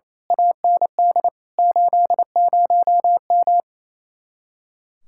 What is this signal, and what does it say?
Morse code, heard as a keyed tone, about 31 wpm, 700 Hz. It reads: AND 80M
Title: Speed-Racing Common QSO Phrases - 28wpm